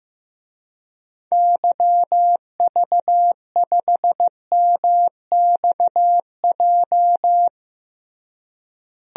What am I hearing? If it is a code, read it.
Morse code, heard as a keyed tone, about 15 wpm, 700 Hz. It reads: YV5MXJ
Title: DXCC-100 Call Signs - 6 Characters - Repeated 2x - 15wpm